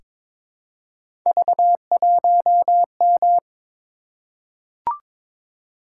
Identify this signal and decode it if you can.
Morse code, heard as a keyed tone, about 22 wpm, 700 Hz. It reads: V1M
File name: practice / 22wpm / Three Letter-Number Combinations - 22wpm